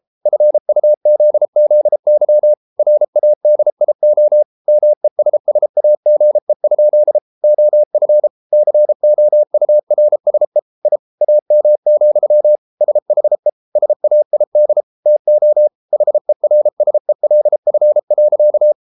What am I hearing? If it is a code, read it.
Morse code, heard as a keyed tone, about 33 wpm, 600 Hz. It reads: FUZZY RADIO MESSAGE? OF COURSE I AM, SHE SAID TO HERSELF.